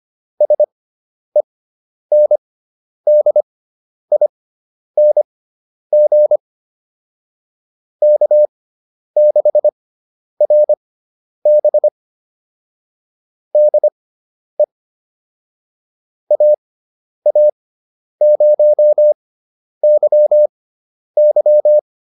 Morse code, heard as a keyed tone, about 25 wpm, spaced as for 10 wpm, 600 Hz. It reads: SENDING K6RB DE AA0YY